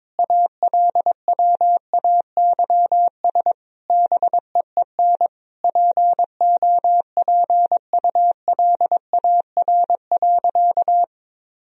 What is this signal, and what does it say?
Morse code, heard as a keyed tone, about 22 wpm, 700 Hz. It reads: ALWAYS BEEN POPULAR.
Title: Sentences from Top 1000 Words - Review Part 3 - 22wpm